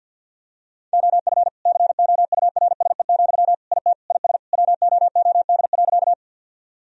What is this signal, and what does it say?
Morse code, heard as a keyed tone, about 50 wpm, 700 Hz. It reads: OF COURSE, IT IS WOOD.
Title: Sentences from Top 1000 Words - Review Part 2 - 50wpm